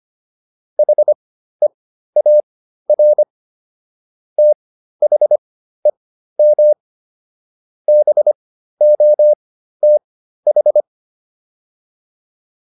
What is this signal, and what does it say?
Morse code, heard as a keyed tone, about 25 wpm, spaced as for 13 wpm, 600 Hz. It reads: HEAR THEM BOTH